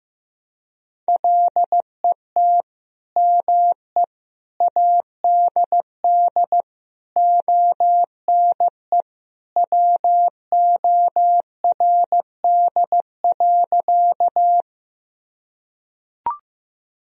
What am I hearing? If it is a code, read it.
Morse code, heard as a keyed tone, about 15 wpm, 700 Hz. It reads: LET ME ADD ONE WORD.
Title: Sentences from Top 1000 Words - Review Part 1 - 15wpm